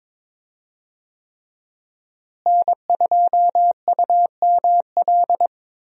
Morse code, heard as a keyed tone, about 22 wpm, 700 Hz. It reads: N2UML
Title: USA Call Signs - 5 Characters - Repeated 3x - 22wpm